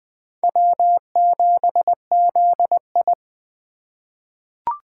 Morse code, heard as a keyed tone, about 20 wpm, 700 Hz. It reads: W7ZI
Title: USA Call Signs - 4 Characters - 2x Character Spacing - 20wpm